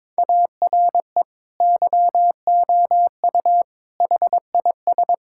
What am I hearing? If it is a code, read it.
Morse code, heard as a keyed tone, about 22 wpm, 700 Hz. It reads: ARE YOU HIS
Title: Sentences from Top 1600 Words 22wpm